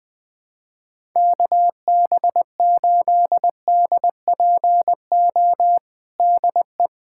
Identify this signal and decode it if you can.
Morse code, heard as a keyed tone, about 20 wpm, 700 Hz. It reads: KB8DPO DE
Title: USA Callsign DE Callsign - 6 Characters - 20wpm